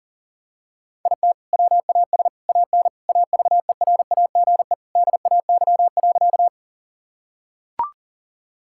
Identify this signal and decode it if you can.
Morse code, heard as a keyed tone, about 37 wpm, 700 Hz. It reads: IT WAS AN AVERAGE DAY.
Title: Sentences from Top 1000 Words - Review Part 6 - 40wpm